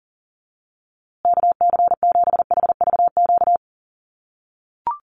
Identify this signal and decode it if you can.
Morse code, heard as a keyed tone, about 40 wpm, 700 Hz. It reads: KC7HVQ
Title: USA Call Signs Repeated 3x 40wpm